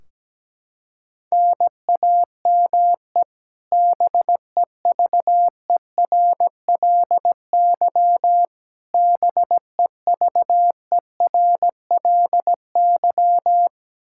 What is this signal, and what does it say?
Morse code, heard as a keyed tone, about 17 wpm, 700 Hz. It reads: NAME BEVERLY BEVERLY
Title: Practice QSO - Exchange 1 - Beginning - 17wpm